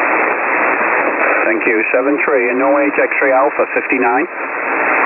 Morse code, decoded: none